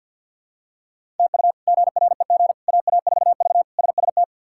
Morse code, heard as a keyed tone, about 52 wpm, 700 Hz. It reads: TU GREG AA4V SST
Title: K1USN SST QSO Practice - 50wpm